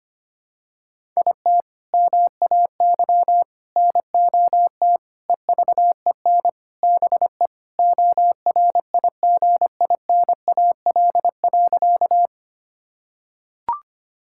Morse code, heard as a keyed tone, about 25 wpm, 700 Hz. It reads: IT MAY NOT EVEN BE ORIGINAL.
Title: Sentences from Top 1000 Words - Review Part 6 - 25wpm